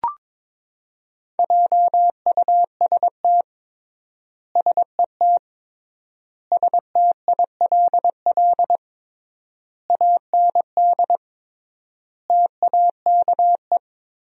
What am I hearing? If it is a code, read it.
Morse code, heard as a keyed tone, about 22 wpm, 700 Hz. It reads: JUST SET STILL AND TAKE
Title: 3x Word-Spacing - Sentences from Top 300 Words - 22wpm